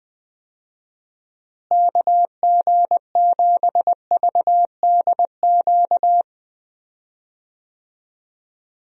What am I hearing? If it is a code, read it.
Morse code, heard as a keyed tone, about 20 wpm, 700 Hz. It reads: KG7VDQ
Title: USA Call Signs - 6 Characters - Repeated 2x - 20wpm